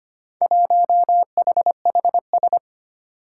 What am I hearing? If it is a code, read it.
Morse code, heard as a keyed tone, about 25 wpm, 700 Hz. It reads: 1HHS
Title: Four Challenging Letter-Number Combinations 25wpm